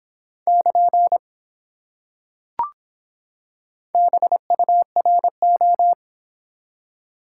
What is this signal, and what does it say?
Morse code, heard as a keyed tone, about 24 wpm, 700 Hz. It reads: <KN> BURO